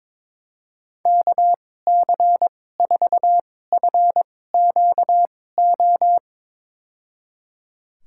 Morse code, heard as a keyed tone, about 22 wpm, spaced as for 16 wpm, 700 Hz. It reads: KC4FQO
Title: USA Call Signs - 6 Characters - 2x Character Spacing - 22wpm